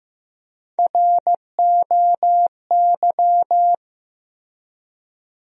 Morse code, heard as a keyed tone, about 15 wpm, 700 Hz. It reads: ROY
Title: USA Common Names - Max 3 Letters - 15wpm